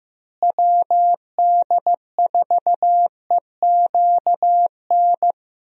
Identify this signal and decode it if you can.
Morse code, heard as a keyed tone, about 16 wpm, 700 Hz. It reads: WD4EQN
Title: USA Call Signs - 6 Characters - Repeated 3x - 15wpm